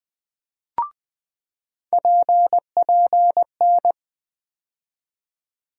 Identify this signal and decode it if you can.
Morse code, heard as a keyed tone, about 20 wpm, 700 Hz. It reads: PPN
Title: USA Call Sign Postfixes - 3 Letters 20wpm